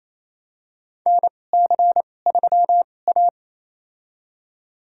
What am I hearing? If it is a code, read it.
Morse code, heard as a keyed tone, about 28 wpm, spaced as for 20 wpm, 700 Hz. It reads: NC3A
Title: USA Call Signs - 2x Character Spacing - 28wpm